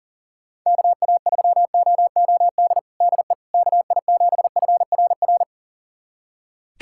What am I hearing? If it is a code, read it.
Morse code, heard as a keyed tone, about 40 wpm, 700 Hz. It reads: KA3OOD DE KI7FRR
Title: USA Callsign DE Callsign 40wpm